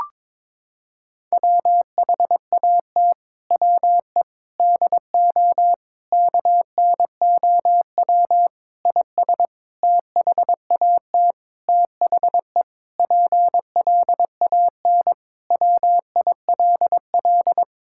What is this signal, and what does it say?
Morse code, heard as a keyed tone, about 22 wpm, 700 Hz. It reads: WHAT WE DO KNOW IS THAT THE PLAN WILL
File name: Sentences with 7-Word Phrases - 22wpm